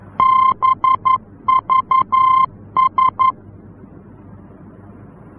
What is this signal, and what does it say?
Morse code, heard as a keyed tone, about 11 wpm, 1040 Hz. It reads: BVS
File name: france / mx / BVS-F-VOR